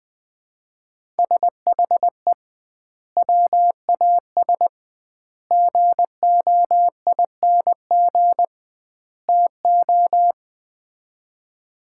Morse code, heard as a keyed tone, about 21 wpm, 700 Hz. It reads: SHE WAS GOING TO
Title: 2x Word-Spacing - Sets of 4 Words - 20wpm